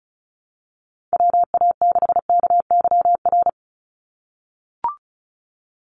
Morse code, heard as a keyed tone, about 34 wpm, 700 Hz. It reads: WA6KYR
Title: NA SOTA Chaser Callsigns 35wpm